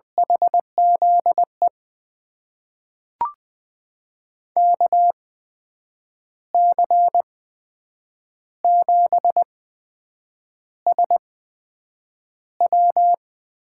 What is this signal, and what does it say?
Morse code, heard as a keyed tone, about 20 wpm, spaced as for 7 wpm, 700 Hz. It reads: HZE KC7SW
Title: USA Call Signs - 6 Characters - 8x Character Spacing - 20wpm